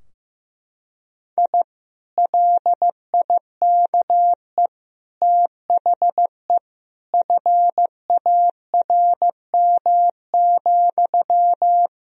Morse code, heard as a keyed tone, about 16 wpm, 700 Hz. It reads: I LIKE THE FARM,
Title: Sentences from Top 1000 Words - Review Part 2 - 15wpm